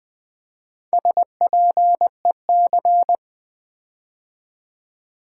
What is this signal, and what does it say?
Morse code, heard as a keyed tone, about 20 wpm, 700 Hz. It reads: SPEC
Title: Build A Word - 4 Letters At A Time - 8-Letter Words - 20wpm